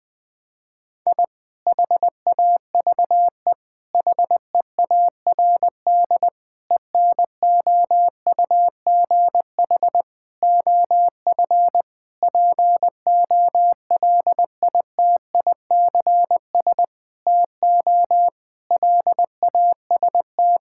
Morse code, heard as a keyed tone, about 20 wpm, 700 Hz. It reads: I HAVE HEARD ENOUGH OF POLITICS TO LAST
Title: Sentences from Top 1000 Words - Review Part 6 - 20wpm